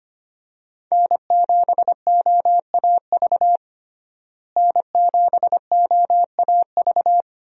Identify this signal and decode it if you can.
Morse code, heard as a keyed tone, about 25 wpm, 700 Hz. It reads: N7OAV N7OAV
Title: USA Call Signs - 5 Characters - Repeated 2x - 25wpm